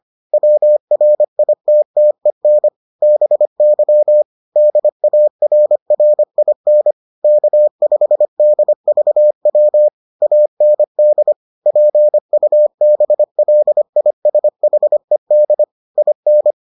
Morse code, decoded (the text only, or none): WRITTEN BY DARRIN K5DVW AND PUBLISHED IN